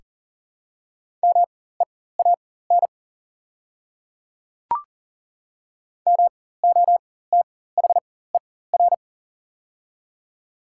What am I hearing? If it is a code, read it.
Morse code, heard as a keyed tone, about 40 wpm, spaced as for 19 wpm, 700 Hz. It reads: MEAN MOTHER
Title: 4x Character Spacing - Top 200 Words - 40wpm